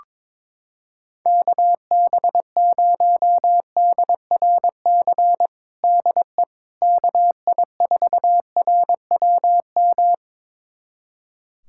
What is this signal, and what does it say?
Morse code, heard as a keyed tone, about 22 wpm, 700 Hz. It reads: KB0DRC DE KI4RWM
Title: USA Callsign DE Callsign - 6 Characters - 22wpm